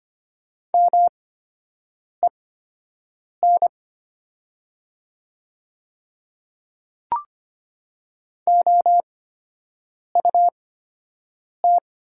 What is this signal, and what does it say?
Morse code, heard as a keyed tone, about 25 wpm, spaced as for 7 wpm, 700 Hz. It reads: MEN OUT
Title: 8x Character Spacing - Top 200 Words - 25wpm